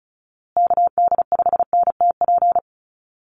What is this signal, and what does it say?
Morse code, heard as a keyed tone, about 34 wpm, 700 Hz. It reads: KD5NTP
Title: USA Call Signs Repeated 2x 35wpm